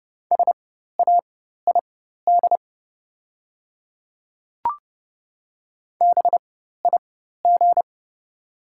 Morse code, heard as a keyed tone, about 30 wpm, spaced as for 14 wpm, 700 Hz. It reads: SAID BIG
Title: 4x Character Spacing - Top 200 Words - 30wpm